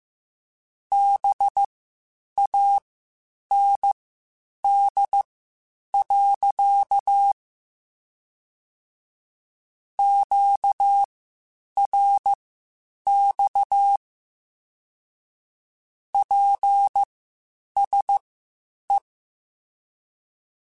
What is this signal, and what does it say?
Morse code, heard as a keyed tone, about 15 wpm, spaced as for 8 wpm, 800 Hz. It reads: BAND. QRX PSE